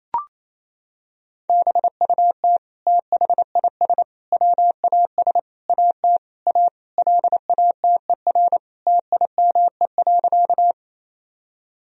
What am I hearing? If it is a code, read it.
Morse code, heard as a keyed tone, about 28 wpm, 700 Hz. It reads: BUT THIS WAS AT A LATER TIME.
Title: Sentences from Top 1000 Words - Review Part 4 - 28wpm